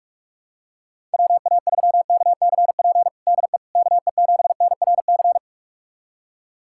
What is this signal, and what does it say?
Morse code, heard as a keyed tone, about 41 wpm, 700 Hz. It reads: WA3KCP DE KE7NRC
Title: USA Callsign DE Callsign - 6 Characters - 45wpm